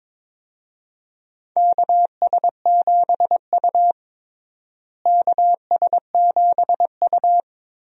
Morse code, heard as a keyed tone, about 22 wpm, 700 Hz. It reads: KS7U KS7U
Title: USA Call Signs - 4 Characters - Repeated 3x - 22wpm